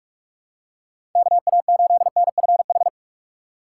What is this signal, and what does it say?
Morse code, heard as a keyed tone, about 46 wpm, 700 Hz. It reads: KA8NFH
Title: USA Call Signs - 6x Character Spacing - 45wpm